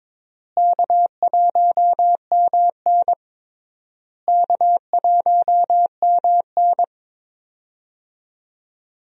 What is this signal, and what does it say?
Morse code, heard as a keyed tone, about 22 wpm, 700 Hz. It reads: K1MN K1MN
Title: USA Call Signs - 4 Characters - Repeated 3x - 22wpm